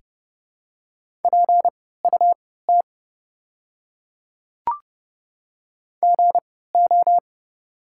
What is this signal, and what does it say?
Morse code, heard as a keyed tone, about 29 wpm, spaced as for 17 wpm, 700 Hz. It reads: PUT GO
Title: 3x Character Spacing - Top 200 Words - 30wpm